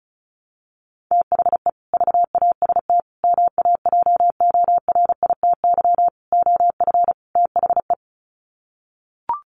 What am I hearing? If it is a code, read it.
Morse code, heard as a keyed tone, about 34 wpm, 700 Hz. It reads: THE VAST MAJORITY OF THE
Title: Sets of 5 Words Encore 35wpm